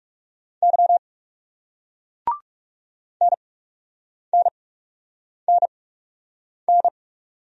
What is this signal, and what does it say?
Morse code, heard as a keyed tone, about 38 wpm, spaced as for 8 wpm, 700 Hz. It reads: Y NNNN